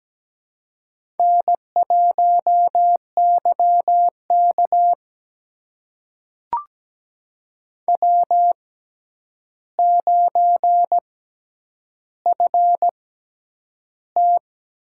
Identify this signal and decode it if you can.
Morse code, heard as a keyed tone, about 17 wpm, spaced as for 9 wpm, 700 Hz. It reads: N1YK W9FT
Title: USA Call Signs - 4 Characters - 6x Character Spacing - 17wpm